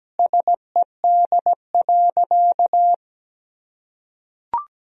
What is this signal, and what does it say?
Morse code, heard as a keyed tone, about 17 wpm, 700 Hz. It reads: SED.
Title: Sentences from Top 200 Words and Abbreviations 17wpm